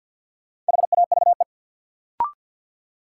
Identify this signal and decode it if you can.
Morse code, heard as a keyed tone, about 48 wpm, 700 Hz. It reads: HAVE
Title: Sentences from Top 100 Words - New Word Drill - 50wpm